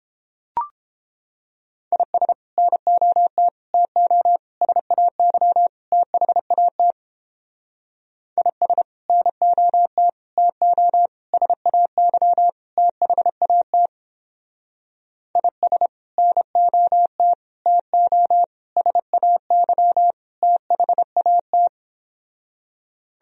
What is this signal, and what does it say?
Morse code, heard as a keyed tone, about 29 wpm, 700 Hz. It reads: IS NOT TO SAY THAT IS NOT TO SAY THAT IS NOT TO SAY THAT